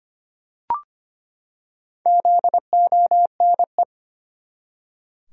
Morse code, heard as a keyed tone, about 25 wpm, 700 Hz. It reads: ZONE
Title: Four Letter Words - 25wpm